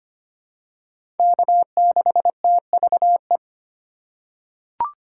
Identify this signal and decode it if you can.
Morse code, heard as a keyed tone, about 25 wpm, 700 Hz. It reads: K6TVE
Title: USA Call Signs - 5 Characters - 8x Character Spacing - 25wpm